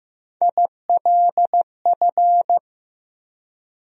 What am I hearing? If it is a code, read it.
Morse code, heard as a keyed tone, about 15 wpm, 700 Hz. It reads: ILF
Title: USA Call Sign Postfixes - 3 Letters 15wpm